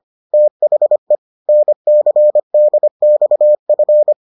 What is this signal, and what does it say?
Morse code, heard as a keyed tone, about 25 wpm, 600 Hz. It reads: THE NCDXF